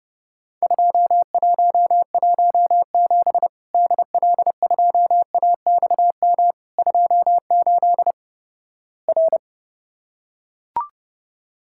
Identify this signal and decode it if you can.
Morse code, heard as a keyed tone, about 31 wpm, 700 Hz. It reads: 2117 DL2AXM 28 R
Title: Worked All Europe DX Contest - QTC Exchange - 30wpm